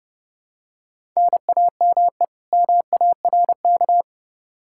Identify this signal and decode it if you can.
Morse code, heard as a keyed tone, about 30 wpm, 700 Hz. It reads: NAME MARK
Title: Practice QSO - Exchange 1 - Beginning - 30wpm